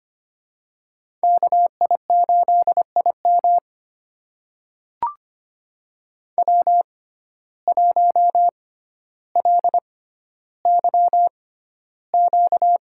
Morse code, heard as a keyed tone, about 25 wpm, spaced as for 13 wpm, 700 Hz. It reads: KI8IM W1LYQ
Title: USA Call Signs - 5 Characters - 6x Character Spacing - 25wpm